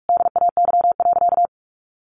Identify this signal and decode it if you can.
Morse code, heard as a keyed tone, about 44 wpm, 700 Hz. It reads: DAY.
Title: Sentences from Top 1200 Words 45wpm